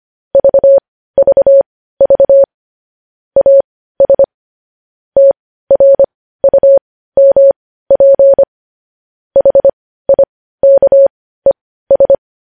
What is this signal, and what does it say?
Morse code, heard as a keyed tone, about 25 wpm, spaced as for 15 wpm, 550 Hz. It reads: VVV AS TRUMP HIKES
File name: podcast / News headlines-1x-25x15